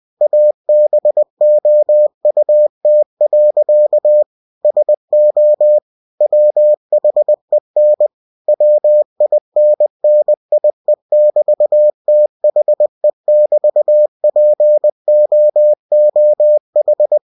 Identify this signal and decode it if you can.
Morse code, heard as a keyed tone, about 20 wpm, 600 Hz. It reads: ABOUT. SO WHEN WINNIE<BT>THE<BT>POOH
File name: cw-books / winnie-the-pooh / winnie-20wpm-page-0026